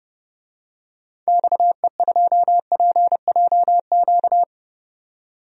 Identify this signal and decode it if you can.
Morse code, heard as a keyed tone, about 30 wpm, 700 Hz. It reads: XE2PJQ
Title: DXCC-100 Call Signs - 6 Characters - 30wpm